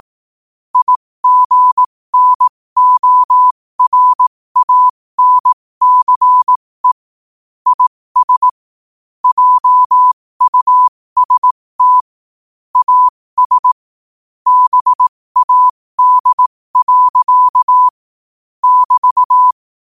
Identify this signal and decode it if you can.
Morse code, heard as a keyed tone, about 18 wpm, spaced as for 15 wpm, 1000 Hz. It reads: IGNORANCE IS JUST AS BAD. <BT>
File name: qotdfiles / 15wpm